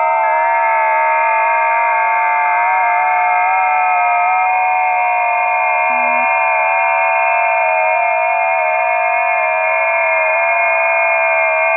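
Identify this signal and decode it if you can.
FT8.
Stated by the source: FT8 test recording from a busy 20 m band.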